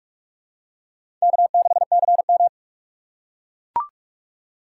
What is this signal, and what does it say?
Morse code, heard as a keyed tone, about 43 wpm, 700 Hz. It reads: K6CM
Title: USA Call Signs - 4x Character Spacing - 45wpm